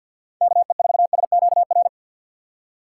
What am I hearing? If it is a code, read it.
Morse code, heard as a keyed tone, about 52 wpm, 700 Hz. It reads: KE4SQR
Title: USA Call Signs - 6 Characters - 3x Character Spacing - 50wpm